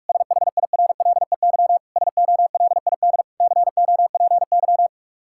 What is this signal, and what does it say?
Morse code, heard as a keyed tone, about 46 wpm, 700 Hz. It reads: SHIRLEY SOLID COPY
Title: Practice QSO - Exchange 3 - More Info - 45wpm